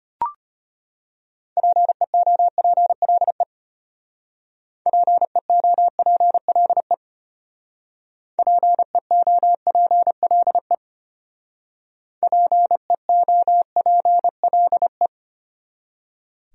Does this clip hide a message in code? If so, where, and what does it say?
Yes: Morse code from 1.6 to 15.1 s, heard as a keyed tone, about 32 wpm, 700 Hz. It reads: PEOPLE PEOPLE PEOPLE PEOPLE